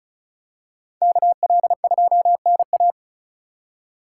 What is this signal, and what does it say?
Morse code, heard as a keyed tone, about 36 wpm, 700 Hz. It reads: KL2NA
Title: USA Call Signs - 5 Characters - 3x Character Spacing - 35wpm